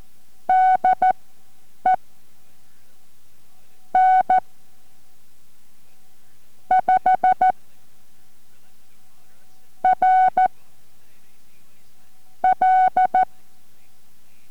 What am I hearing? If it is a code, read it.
Morse code, heard as a keyed tone, about 14 wpm, spaced as for 8 wpm, 750 Hz. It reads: DE N 5 R L